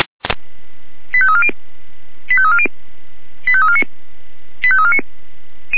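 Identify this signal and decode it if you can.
Five-tone selective call (ZVEI), heard as quick runs of five tones, tones of about 70 ms. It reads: 85340 85340 85349 85349
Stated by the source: ZVEI five-tone selective call received by SDR.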